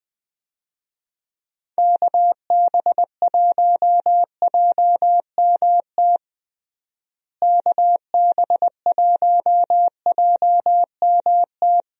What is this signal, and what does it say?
Morse code, heard as a keyed tone, about 20 wpm, 700 Hz. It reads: KB1JMT KB1JMT
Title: USA Call Signs - 6 Characters - Repeated 3x - 20wpm